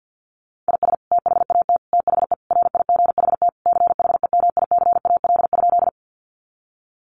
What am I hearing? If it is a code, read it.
Morse code, heard as a keyed tone, about 50 wpm, 700 Hz. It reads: IS THAT THE RIGHT CHEMICAL?
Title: Sentences from Top 1200 Words 50wpm